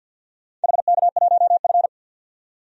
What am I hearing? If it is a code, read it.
Morse code, heard as a keyed tone, about 52 wpm, 700 Hz. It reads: HK1F
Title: DXCC-100 Call Signs - 4 Characters - 50wpm